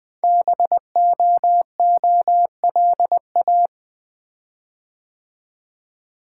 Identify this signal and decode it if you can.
Morse code, heard as a keyed tone, about 20 wpm, 700 Hz. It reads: BOOLA
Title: Code Groups - 5 Letters - 20wpm